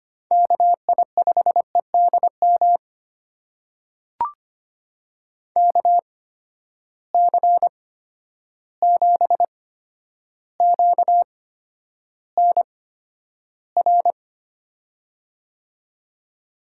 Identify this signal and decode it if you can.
Morse code, heard as a keyed tone, about 25 wpm, spaced as for 11 wpm, 700 Hz. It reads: KI5EDM KC7QNR
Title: USA Call Signs - 6 Characters - 8x Character Spacing - 25wpm